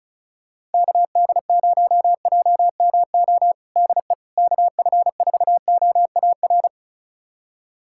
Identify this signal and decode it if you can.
Morse code, heard as a keyed tone, about 36 wpm, 700 Hz. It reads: KD0JMO DE KF4OAR
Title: USA Callsign DE Callsign - 6 Characters - 35wpm